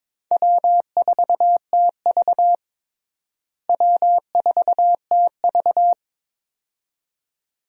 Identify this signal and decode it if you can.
Morse code, heard as a keyed tone, about 22 wpm, 700 Hz. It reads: W4TV W4TV
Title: USA Call Signs - 4 Characters - Repeated 2x - 22wpm